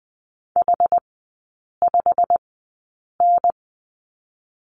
Morse code, heard as a keyed tone, about 20 wpm, spaced as for 8 wpm, 700 Hz. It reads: H5N
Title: 152 Lesson 25e - Letter p - Three Characters 20wpm